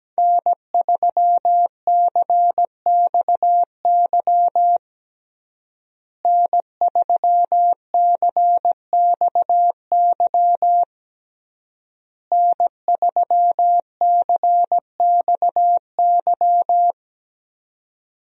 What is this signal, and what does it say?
Morse code, heard as a keyed tone, about 17 wpm, 700 Hz. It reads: N3CXY N3CXY N3CXY
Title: USA Call Signs - 5 Characters - Repeated 3x - 17wpm